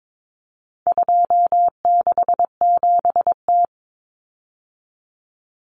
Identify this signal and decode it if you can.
Morse code, heard as a keyed tone, about 22 wpm, 700 Hz. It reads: 267T
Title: Common Cut Numbers - 4-Digit 22wpm